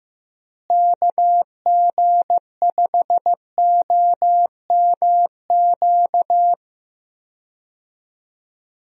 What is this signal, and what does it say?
Morse code, heard as a keyed tone, about 15 wpm, 700 Hz. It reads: KG5OMQ
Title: USA Call Signs - 6 Characters - Repeated 2x - 15wpm